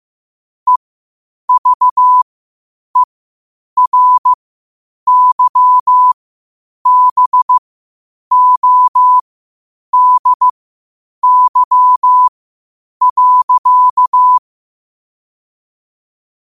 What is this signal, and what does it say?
Morse code, heard as a keyed tone, about 15 wpm, spaced as for 8 wpm, 1000 Hz. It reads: EVERYBODY.
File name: qotdfiles / 8wpm